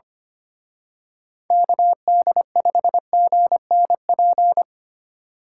Morse code, heard as a keyed tone, about 27 wpm, 700 Hz. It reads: KD5GNP
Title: Speed Racing - USA 2x3 Call Signs 25wpm